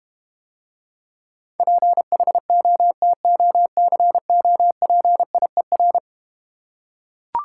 Morse code, heard as a keyed tone, about 32 wpm, 700 Hz. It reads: PHOTOCOPIER